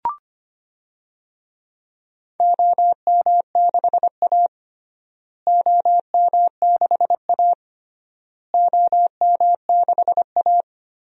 Morse code, heard as a keyed tone, about 25 wpm, 700 Hz. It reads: OM6A OM6A OM6A
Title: DXCC-100 Call Signs - 4 Characters - Repeated 3x - 25wpm